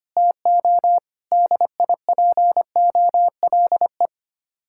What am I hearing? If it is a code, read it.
Morse code, heard as a keyed tone, about 31 wpm, 700 Hz. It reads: TO DIPOLE
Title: Speed-Racing Common QSO Phrases Encore - 25wpm